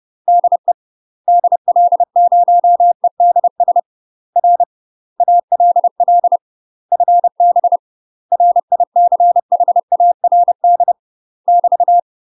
Morse code, heard as a keyed tone, about 30 wpm, 700 Hz. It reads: DE DL0EDS R ALL FB RICHARD <BT>
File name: UK MASTER QSOs TRAINING / 30WPM / UK MASTER QSOs 0048